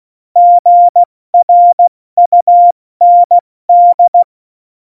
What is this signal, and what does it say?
Morse code, heard as a keyed tone, about 16 wpm, 700 Hz. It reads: GRUND
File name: Morsekurs / Witze / Witz 13. Februar 2018, 70 F80